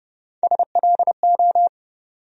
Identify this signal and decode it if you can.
Morse code, heard as a keyed tone, about 30 wpm, 700 Hz. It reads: SLO
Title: USA Call Sign Postfixes - 3 Letters 30wpm